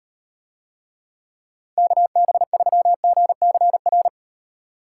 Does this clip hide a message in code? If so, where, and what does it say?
Yes: Morse code from 1.8 to 4.1 s, heard as a keyed tone, about 38 wpm, 700 Hz. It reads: KB3GCR